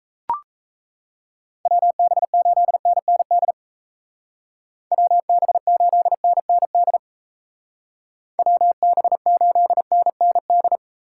Morse code, heard as a keyed tone, about 36 wpm, 700 Hz. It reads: WB8NND WB8NND WB8NND